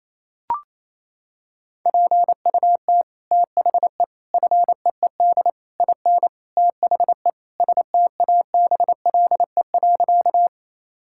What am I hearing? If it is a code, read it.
Morse code, heard as a keyed tone, about 28 wpm, 700 Hz. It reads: PUT THE FEED IN THE STABLE.
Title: Sentences from Top 1600 Words 28wpm